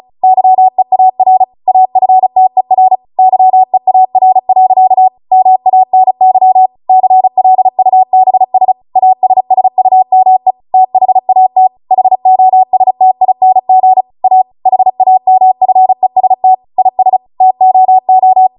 Morse code, heard as a keyed tone, about 36 wpm, 750 Hz. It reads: YEAR AFTER YEAR. MANY CLUBS ASSUME THAT HOSTING A HAMFEST IS TOO